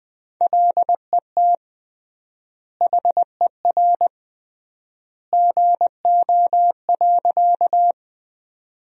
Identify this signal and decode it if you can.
Morse code, heard as a keyed tone, about 20 wpm, 700 Hz. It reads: LET HER GO.
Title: 3x Word-Spacing - Sentences from Top 300 Words - 20wpm